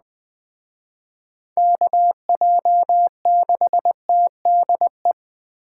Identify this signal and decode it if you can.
Morse code, heard as a keyed tone, about 20 wpm, 700 Hz. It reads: KJ6TDE
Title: Speed Racing - USA 2x3 Call Signs 17wpm